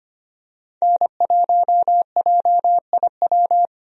Morse code, heard as a keyed tone, about 25 wpm, 700 Hz. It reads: N1JIW
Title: USA Call Signs - 5 Characters - Repeated 3x - 25wpm